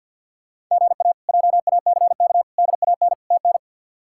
Morse code, heard as a keyed tone, about 49 wpm, 700 Hz. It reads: GA JACK DAN TN
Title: K1USN SST QSO Practice - 50wpm